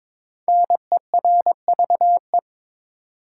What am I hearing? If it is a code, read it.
Morse code, heard as a keyed tone, about 22 wpm, 700 Hz. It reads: NERVE
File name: 5 Letter Words - First 3 Letters Given - 22wpm